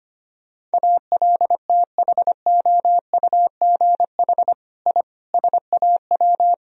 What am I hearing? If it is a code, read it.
Morse code, heard as a keyed tone, about 25 wpm, 700 Hz. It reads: ALTHOUGH I SAW
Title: Sentences from Top 1000 Words - Review Part 3 - 25wpm